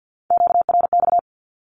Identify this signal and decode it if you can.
Morse code, heard as a keyed tone, about 48 wpm, 700 Hz. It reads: QRX
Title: Q-Codes Repeated 3x 50wpm